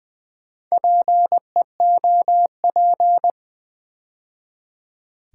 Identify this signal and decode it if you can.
Morse code, heard as a keyed tone, about 20 wpm, 700 Hz. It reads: PEOP
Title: Build A Word - 2 Letters At A Time - 6-Letter Words - 20wpm